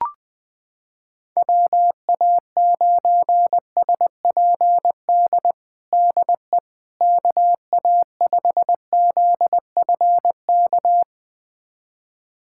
Morse code, heard as a keyed tone, about 21 wpm, 700 Hz. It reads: WA9SPD DE KA5ZFK
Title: USA Callsign DE Callsign - 6 Characters - 20wpm